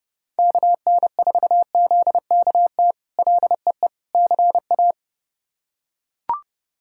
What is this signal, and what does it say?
Morse code, heard as a keyed tone, about 30 wpm, 700 Hz. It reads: KN4ZKT LEE CA
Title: K1USN SST QSO Practice - 30wpm